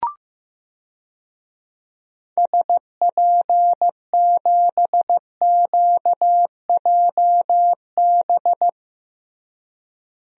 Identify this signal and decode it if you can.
Morse code, heard as a keyed tone, about 15 wpm, 700 Hz. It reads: SP7QJB
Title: DXCC-100 Call Signs - 6 Characters - Repeated 3x - 15wpm